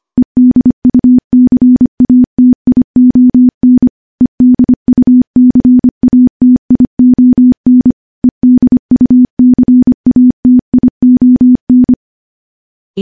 Morse code, heard as a keyed tone, about 25 wpm, 250 Hz. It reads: EDUCATION EDUCATION EDUCATION
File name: morse / morse-training-25WPM